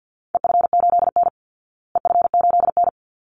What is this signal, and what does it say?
Morse code, heard as a keyed tone, about 49 wpm, 700 Hz. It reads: EF8N EF8N
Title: Call Signs Repeated 3x - 50wpm